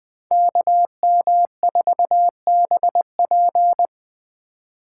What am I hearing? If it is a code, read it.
Morse code, heard as a keyed tone, about 21 wpm, 700 Hz. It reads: KM4BP
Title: Speed Racing - USA 2x2 Call Signs 17wpm